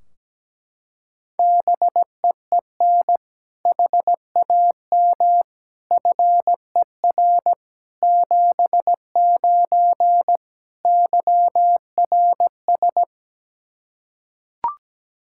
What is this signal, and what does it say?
Morse code, heard as a keyed tone, about 17 wpm, 700 Hz. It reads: BEEN HAM FER 79 YRS
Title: Practice QSO - Exchange 3 - More Info - 17wpm